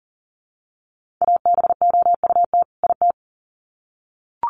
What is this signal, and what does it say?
Morse code, heard as a keyed tone, about 40 wpm, 700 Hz. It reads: ABOUT IT
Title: Sets of 2 Words 40wpm